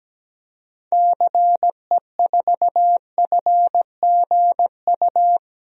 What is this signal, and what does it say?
Morse code, heard as a keyed tone, about 17 wpm, 700 Hz. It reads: CE4FGU
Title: DXCC-100 Call Signs - 6 Characters - 17wpm